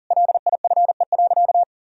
Morse code, heard as a keyed tone, about 40 wpm, 700 Hz. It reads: LIFE.
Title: Sentences from Top 1000 Words - Review Part 4 - 40wpm